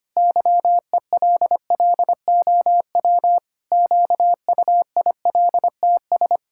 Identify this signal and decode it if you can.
Morse code, heard as a keyed tone, about 25 wpm, 700 Hz. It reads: YELLOW QUILTS
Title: Pangram Challenge v2 - 25wpm